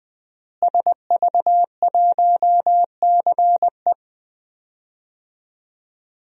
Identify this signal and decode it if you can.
Morse code, heard as a keyed tone, about 20 wpm, 700 Hz. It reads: SV1CE
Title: DXCC-100 Call Signs - 5 Characters - Repeated 2x - 20wpm